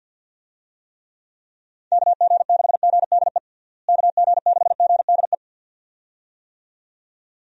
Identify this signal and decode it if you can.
Morse code, heard as a keyed tone, about 50 wpm, 700 Hz. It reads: KG6GDE KG6GDE
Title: USA Call Signs - 6 Characters - Repeated 2x - 50wpm